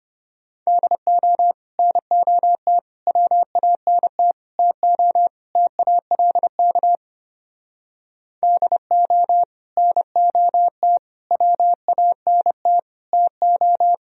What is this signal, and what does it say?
Morse code, heard as a keyed tone, about 31 wpm, 700 Hz. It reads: DO NOT WANT TO TALK DO NOT WANT TO